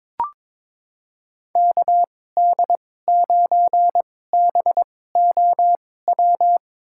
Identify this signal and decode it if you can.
Morse code, heard as a keyed tone, about 22 wpm, spaced as for 16 wpm, 700 Hz. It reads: KD9BOW
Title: USA Call Signs - 2x Character Spacing - 22wpm